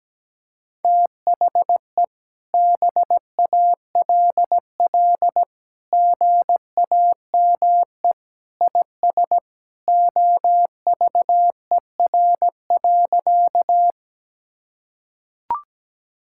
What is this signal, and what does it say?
Morse code, heard as a keyed tone, about 17 wpm, 700 Hz. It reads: THE BALL GAME IS OVER.
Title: Sentences from Top 1000 Words - Review Part 6 - 17wpm